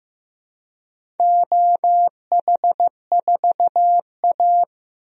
Morse code, heard as a keyed tone, about 15 wpm, 700 Hz. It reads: OH4A
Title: DXCC-100 Call Signs - 4 Characters - 15wpm